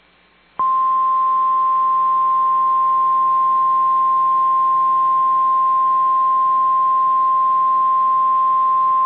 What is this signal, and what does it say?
Emergency Alert System (SAME) tones (the 1050 Hz attention tone). It is heard from 0.6 to 8.1 s.